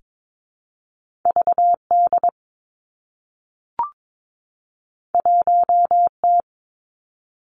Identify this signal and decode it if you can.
Morse code, heard as a keyed tone, about 22 wpm, 700 Hz. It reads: V D 1 T
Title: Two Challenging Letter-Number Combinations 22wpm